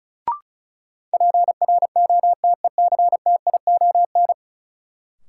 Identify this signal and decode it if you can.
Morse code, heard as a keyed tone, about 36 wpm, 700 Hz. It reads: PROTECTION
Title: Sentences from Top 1600 Words - New Word Drill - 35wpm